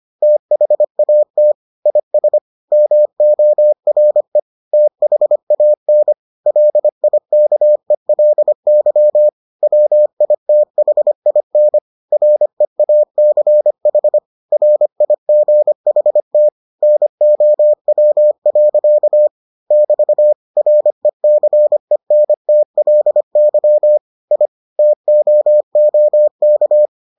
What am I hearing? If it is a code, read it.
Morse code, heard as a keyed tone, about 25 wpm, 600 Hz. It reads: THAT IS MORE THAN LIKELY WITHIN REACH RIGHT NOW. <BT> RECENTLY I TOOK